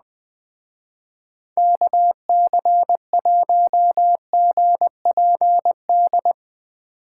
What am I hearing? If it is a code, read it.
Morse code, heard as a keyed tone, about 20 wpm, 700 Hz. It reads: KC1GPD
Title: Speed Racing - USA 2x3 Call Signs 17wpm